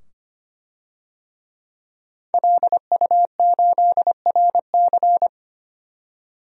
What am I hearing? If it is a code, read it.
Morse code, heard as a keyed tone, about 25 wpm, 700 Hz. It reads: LU8RC
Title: DXCC-100 Call Signs - 5 Characters - 25wpm